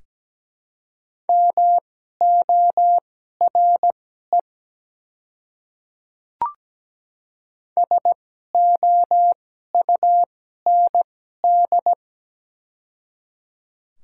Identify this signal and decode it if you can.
Morse code, heard as a keyed tone, about 17 wpm, spaced as for 12 wpm, 700 Hz. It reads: MORE SOUND
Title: 2x Character Spacing - Top 200 Words - 17wpm